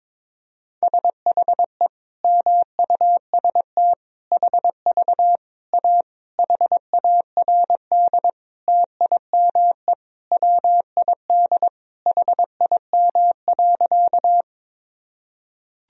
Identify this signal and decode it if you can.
Morse code, heard as a keyed tone, about 22 wpm, 700 Hz. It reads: SHE MUST HV A HARD TIME WID HIM.
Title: Sentences from Top 300 Words and Abbreviations 22wpm